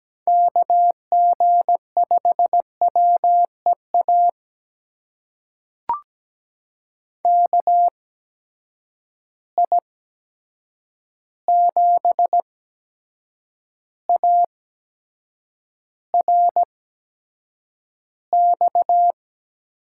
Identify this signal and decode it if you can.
Morse code, heard as a keyed tone, about 17 wpm, spaced as for 7 wpm, 700 Hz. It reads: KG5WEA KI7ARX
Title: USA Call Signs - 6 Characters - 8x Character Spacing - 17wpm